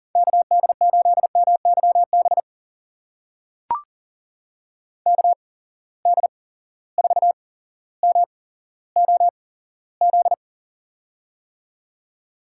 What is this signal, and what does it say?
Morse code, heard as a keyed tone, about 42 wpm, spaced as for 17 wpm, 700 Hz. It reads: KD8MYB KD4MOZ
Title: USA Call Signs - 6 Characters - 8x Character Spacing - 40wpm